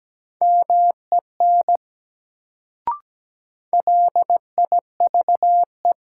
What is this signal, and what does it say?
Morse code, heard as a keyed tone, about 17 wpm, 700 Hz. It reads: MEN LIVE
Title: Sentences from Top 200 Words - New Word Drill - 17wpm